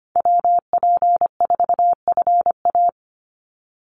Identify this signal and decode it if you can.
Morse code, heard as a keyed tone, about 25 wpm, 700 Hz. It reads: WP4FA
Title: USA Call Signs 25wpm